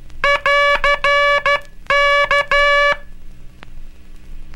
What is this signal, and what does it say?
Morse code, heard as a keyed tone, about 13 wpm, 1700 Hz, with noise. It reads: <AR> K
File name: cursos / Curso CW mp3 / CW Aula - 18